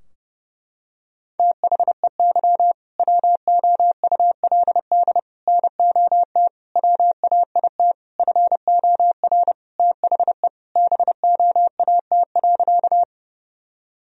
Morse code, heard as a keyed tone, about 30 wpm, 700 Hz. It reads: THEY WOULD NOT WAIT FOR THE BOAT.
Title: Sentences from Top 1000 Words - Review Part 3 - 30wpm